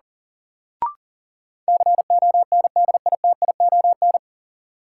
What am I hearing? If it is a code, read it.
Morse code, heard as a keyed tone, about 38 wpm, 700 Hz. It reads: CONDITION
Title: Sentences from Top 1000 Words - New Word Drill - 40wpm